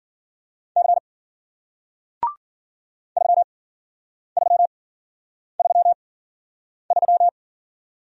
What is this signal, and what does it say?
Morse code, heard as a keyed tone, about 54 wpm, spaced as for 9 wpm, 700 Hz. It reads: C 3333